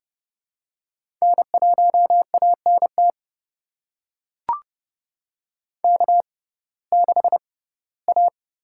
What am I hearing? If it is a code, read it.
Morse code, heard as a keyed tone, about 30 wpm, spaced as for 18 wpm, 700 Hz. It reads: N1ANT K6A
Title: USA Call Signs - 6x Character Spacing - 30wpm